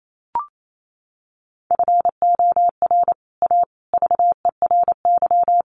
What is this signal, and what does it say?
Morse code, heard as a keyed tone, about 28 wpm, 700 Hz. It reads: FOR A VERY
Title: Sets of 5 Words 28wpm